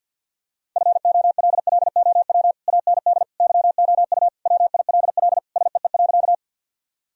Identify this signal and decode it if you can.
Morse code, heard as a keyed tone, about 53 wpm, 700 Hz. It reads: FOLLOW AND YOU WILL SEE.
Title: Sentences from Top 1000 Words - Review Part 1 - 50wpm